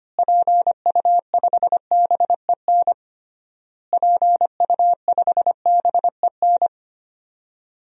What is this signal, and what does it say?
Morse code, heard as a keyed tone, about 25 wpm, 700 Hz. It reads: PU5BEN PU5BEN
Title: DXCC-100 Call Signs - 6 Characters - Repeated 3x - 25wpm